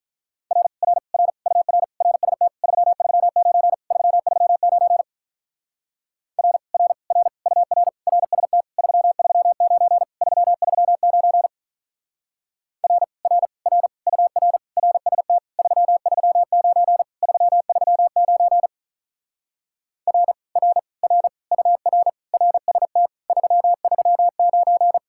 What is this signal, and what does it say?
Morse code, heard as a keyed tone, about 46 wpm, 700 Hz. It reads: R R R UR RST 339 339 R R R UR RST 339 339 R R R UR RST 339 339 R R R UR RST 339